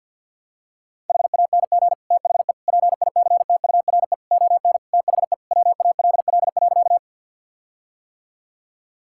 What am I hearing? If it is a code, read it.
Morse code, heard as a keyed tone, about 50 wpm, 700 Hz. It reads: HANG THE PICTURE ON THE WALL.
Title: Sentences from Top 1600 Words 50wpm